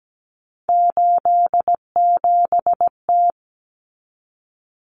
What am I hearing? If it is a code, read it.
Morse code, heard as a keyed tone, about 17 wpm, 700 Hz. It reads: 87T
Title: Common Cut Numbers - 3-Digit 17wpm